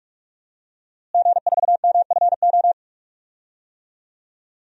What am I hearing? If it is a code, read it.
Morse code, heard as a keyed tone, about 46 wpm, 700 Hz. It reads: G4MFO
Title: DXCC-100 Call Signs - 5 Characters - Repeated 2x - 45wpm